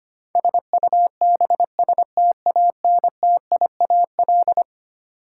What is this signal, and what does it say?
Morse code, heard as a keyed tone, about 25 wpm, 700 Hz. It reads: SUBSTANTIAL
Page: morse-code-ninja-practice-25wpm